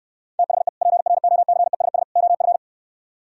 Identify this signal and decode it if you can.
Morse code, heard as a keyed tone, about 67 wpm, 700 Hz. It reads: THE PROCESS OF